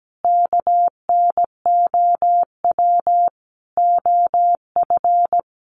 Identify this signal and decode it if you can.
Morse code, heard as a keyed tone, about 17 wpm, 700 Hz. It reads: KNOW OF
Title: Sentences from Top 1200 Words 17wpm